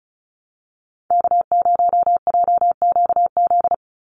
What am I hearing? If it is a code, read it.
Morse code, heard as a keyed tone, about 34 wpm, 700 Hz. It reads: K0JQZ
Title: NA SOTA Activator Callsigns 35wpm